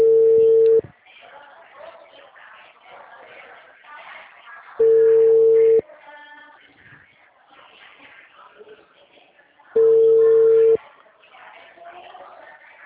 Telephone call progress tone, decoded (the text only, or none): ringback tone